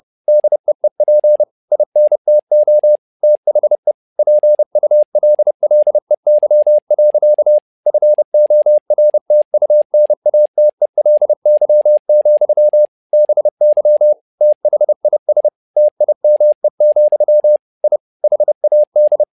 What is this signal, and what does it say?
Morse code, heard as a keyed tone, about 30 wpm, 600 Hz. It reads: DEEP INTO THE PULLEY. FORTUNATELY, BY THIS TIME, I HAD
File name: learning center / tower trouble / tower trouble 30wpm